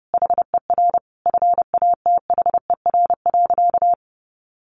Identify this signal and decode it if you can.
Morse code, heard as a keyed tone, about 30 wpm, 700 Hz. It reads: HER FATHER.
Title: Sentences from Top 1500 Words 30wpm